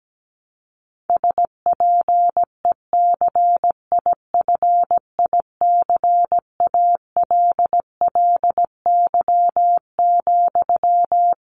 Morse code, heard as a keyed tone, about 17 wpm, 700 Hz. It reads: SPECIFICALLY,
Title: Sentences from Top 1100 Words 17wpm